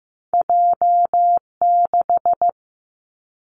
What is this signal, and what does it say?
Morse code, heard as a keyed tone, about 15 wpm, 700 Hz. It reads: J 6
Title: Two Challenging Letter-Number Combinations 15wpm